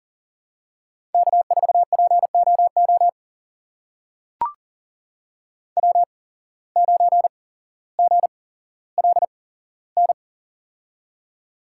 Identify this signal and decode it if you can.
Morse code, heard as a keyed tone, about 40 wpm, spaced as for 17 wpm, 700 Hz. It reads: K4POO W9GLN
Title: USA Call Signs - 5 Characters - 8x Character Spacing - 40wpm